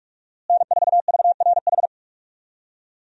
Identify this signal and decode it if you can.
Morse code, heard as a keyed tone, about 44 wpm, 700 Hz. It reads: N4VRH
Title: USA Call Signs - 5 Characters - Repeated 2x - 45wpm